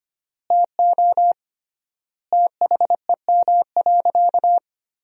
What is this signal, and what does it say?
Morse code, heard as a keyed tone, about 25 wpm, 700 Hz. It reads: TO THEM.
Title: 3x Word-Spacing - Sentences from Top 200 Words - 25wpm